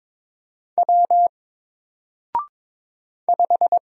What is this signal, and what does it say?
Morse code, heard as a keyed tone, about 22 wpm, 700 Hz. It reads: W 5
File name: Single-Letter-Number-22wpm